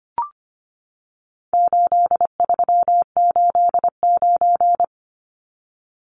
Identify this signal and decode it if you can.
Morse code, heard as a keyed tone, about 25 wpm, 700 Hz. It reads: 8389
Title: Numbers - 4-Digit 25wpm